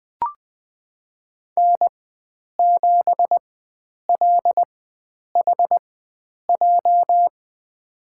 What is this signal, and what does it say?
Morse code, heard as a keyed tone, about 20 wpm, spaced as for 9 wpm, 700 Hz. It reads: N7LHJ
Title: USA Call Signs - 5 Characters - 4x Character Spacing - 20wpm